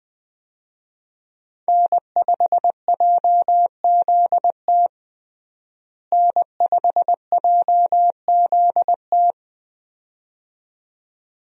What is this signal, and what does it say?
Morse code, heard as a keyed tone, about 21 wpm, 700 Hz. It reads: N5JZT N5JZT
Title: USA Call Signs - 5 Characters - Repeated 2x - 20wpm